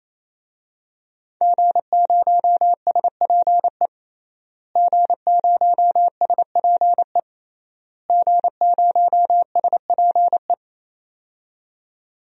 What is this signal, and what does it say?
Morse code, heard as a keyed tone, about 27 wpm, 700 Hz. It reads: G0SPE G0SPE G0SPE
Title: DXCC-100 Call Signs - 5 Characters - Repeated 3x - 28wpm